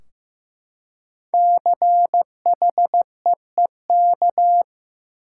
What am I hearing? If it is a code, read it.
Morse code, heard as a keyed tone, about 15 wpm, 700 Hz. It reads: CHEEK
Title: Five Letter Words - 15wpm